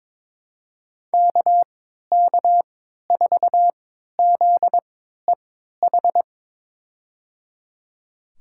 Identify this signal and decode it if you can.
Morse code, heard as a keyed tone, about 22 wpm, spaced as for 13 wpm, 700 Hz. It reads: KK4ZEH
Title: USA Call Signs - 6 Characters - 3x Character Spacing - 22wpm